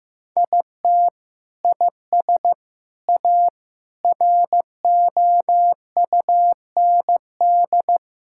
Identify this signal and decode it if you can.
Morse code, heard as a keyed tone, about 16 wpm, 700 Hz. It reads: IT IS A ROUND
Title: Sentences from Top 1000 Words - Review Part 6 - 15wpm